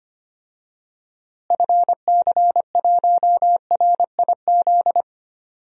Morse code, heard as a keyed tone, about 25 wpm, 700 Hz. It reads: FC1RIZ
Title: DXCC-100 Call Signs - 6 Characters - Repeated 2x - 25wpm